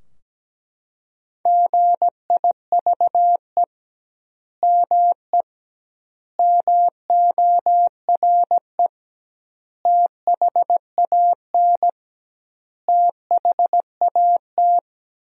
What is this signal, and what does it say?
Morse code, heard as a keyed tone, about 17 wpm, 700 Hz. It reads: GIVE ME MORE THAN THAT
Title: 2x Word-Spacing - Sentences from Top 200 Words - 17wpm